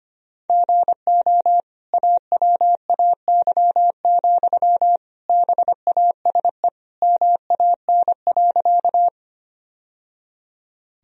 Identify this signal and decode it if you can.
Morse code, heard as a keyed tone, about 25 wpm, 700 Hz. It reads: GO AWAY, BASE MAN.
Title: Sentences from Top 1000 Words - Review Part 1 - 25wpm